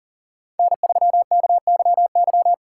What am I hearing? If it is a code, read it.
Morse code, heard as a keyed tone, about 40 wpm, 700 Hz. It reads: N3KYY
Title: USA Call Signs - 4x Character Spacing - 40wpm